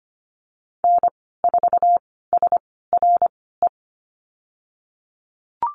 Morse code, heard as a keyed tone, about 25 wpm, spaced as for 16 wpm, 700 Hz. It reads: N4SRE
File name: Lesson 15g - Number 5 - Call Signs 25wpm